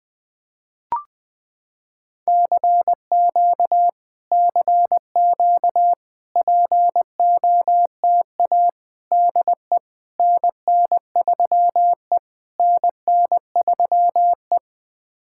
Morse code, heard as a keyed tone, about 20 wpm, 700 Hz. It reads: CQ CQ POTA DE NN3E NN3E
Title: POTA QSO Practice - 20wpm